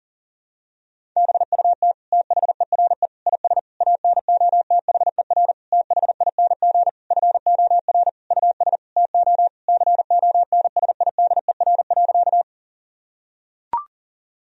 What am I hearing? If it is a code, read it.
Morse code, heard as a keyed tone, about 40 wpm, 700 Hz. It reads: BUT THERE IS ANOTHER THING FOR US TO CONSIDER.
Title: Sentences from Top 1000 Words - Review Part 4 - 40wpm